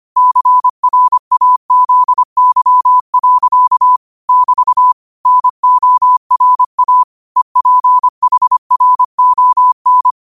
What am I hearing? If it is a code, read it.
Morse code, heard as a keyed tone, about 25 wpm, 1000 Hz. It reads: CRAZY. <BT> NORA EPHRON